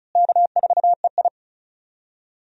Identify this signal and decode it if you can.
Morse code, heard as a keyed tone, about 36 wpm, 700 Hz. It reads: K4EI
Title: USA Call Signs - 4 Characters - Repeated 3x - 35wpm